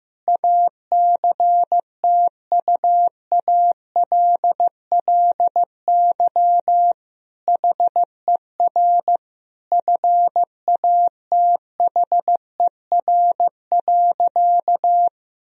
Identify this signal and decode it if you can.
Morse code, heard as a keyed tone, about 15 wpm, 700 Hz. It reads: ACTUALLY HER FATHER.
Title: Sentences from Top 1000 Words - Review Part 4 - 15wpm